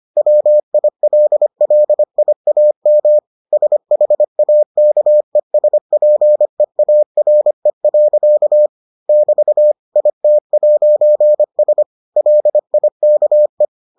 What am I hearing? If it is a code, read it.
Morse code, heard as a keyed tone, about 25 wpm, 600 Hz. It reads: WILLIAM SHAKESPEARE. <BT> IT'S LIKE